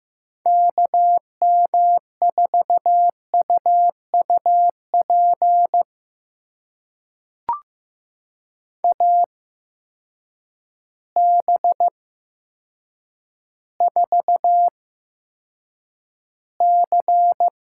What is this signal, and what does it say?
Morse code, heard as a keyed tone, about 15 wpm, spaced as for 7 wpm, 700 Hz. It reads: KM4UUP AB4C
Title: USA Call Signs - 8x Character Spacing - 15wpm